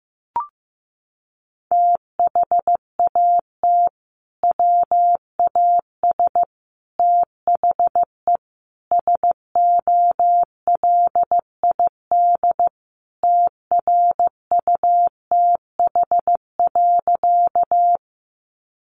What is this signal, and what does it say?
Morse code, heard as a keyed tone, about 15 wpm, 700 Hz. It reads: THAT WAS THE SOLID TRUTH.
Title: Sentences from Top 1500 Words 15wpm